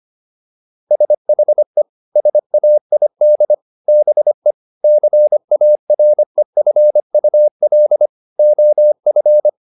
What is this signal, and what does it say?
Morse code, heard as a keyed tone, about 25 wpm, 600 Hz. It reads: SHE SAID BE CAREFUL OF